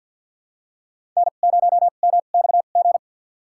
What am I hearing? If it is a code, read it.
Morse code, heard as a keyed tone, about 50 wpm, spaced as for 36 wpm, 700 Hz. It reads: N0MXG
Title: USA Call Signs - 5 Characters - 2x Character Spacing - 50wpm